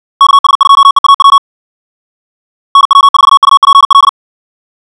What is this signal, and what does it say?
Morse code, heard as a keyed tone, about 56 wpm, 1110 Hz. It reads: KS2EAW NW4UPL
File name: ja7uhv / SampleHST2